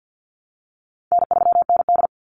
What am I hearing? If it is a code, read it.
Morse code, heard as a keyed tone, about 49 wpm, 700 Hz. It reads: N3ND
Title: Easy Call Signs Repeated 3x - 50wpm